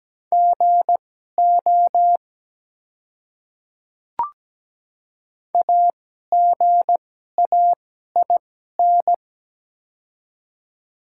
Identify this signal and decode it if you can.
Morse code, heard as a keyed tone, about 17 wpm, spaced as for 12 wpm, 700 Hz. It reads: GO AGAIN
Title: 2x Character Spacing - Top 200 Words - 17wpm